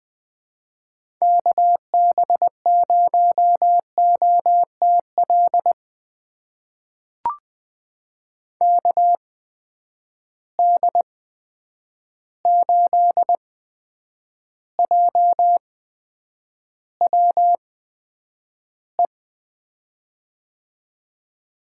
Morse code, heard as a keyed tone, about 21 wpm, spaced as for 9 wpm, 700 Hz. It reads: KB0OTL KD8JWE
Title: USA Call Signs - 6 Characters - 8x Character Spacing - 20wpm